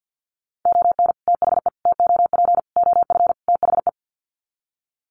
Morse code, heard as a keyed tone, about 50 wpm, 700 Hz. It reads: ON THE TOP OF THE
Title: Sets of 5 Words Encore 50wpm